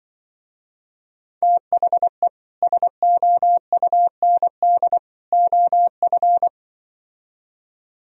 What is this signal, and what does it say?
Morse code, heard as a keyed tone, about 24 wpm, 700 Hz. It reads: THE SOUND OF
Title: Speed-Racing - Sets of 3 Words 20wpm